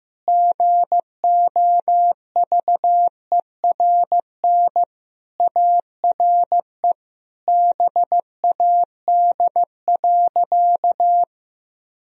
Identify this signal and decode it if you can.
Morse code, heard as a keyed tone, about 15 wpm, 700 Hz. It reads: GOVERN ARE BAD.
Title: Sentences from Top 1000 Words - Review Part 2 - 15wpm